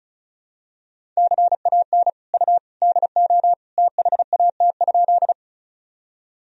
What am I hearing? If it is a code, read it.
Morse code, heard as a keyed tone, about 36 wpm, 700 Hz. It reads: CAN U DO THAT?
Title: Sentences from Top 100 Words and Abbreviations 35wpm